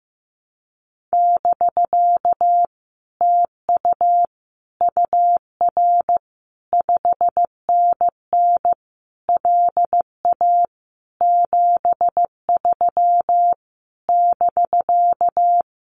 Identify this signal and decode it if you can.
Morse code, heard as a keyed tone, about 15 wpm, 700 Hz. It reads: <BK> TU UR 5NN LA 73 <BK>
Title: Common QSO Phrases Encore 15wpm